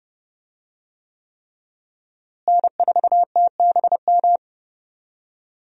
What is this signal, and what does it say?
Morse code, heard as a keyed tone, about 31 wpm, 700 Hz. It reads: N4TBM
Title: USA Call Signs - 5 Characters - Repeated 3x - 30wpm